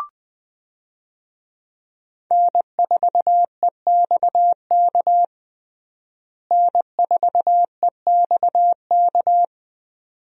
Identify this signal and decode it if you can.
Morse code, heard as a keyed tone, about 21 wpm, 700 Hz. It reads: N4EXK N4EXK
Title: USA Call Signs - 5 Characters - Repeated 2x - 20wpm